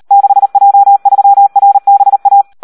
Morse code, heard as a keyed tone, about 38 wpm, 800 Hz. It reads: 6J3RBA
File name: Audio practica CW / SD 1 38